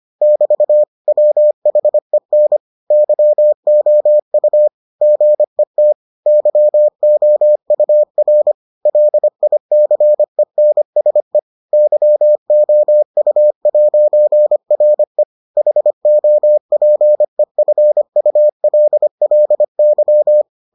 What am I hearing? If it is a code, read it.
Morse code, heard as a keyed tone, about 25 wpm, 600 Hz. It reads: <BT> WHEN YOU GET YOUR LICENSE YOU'RE HOPEFULLY